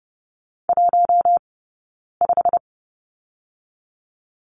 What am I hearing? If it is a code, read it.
Morse code, heard as a keyed tone, about 30 wpm, 700 Hz. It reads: 1 5
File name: Lesson 15d - Number 5 - Two Characters 30wpm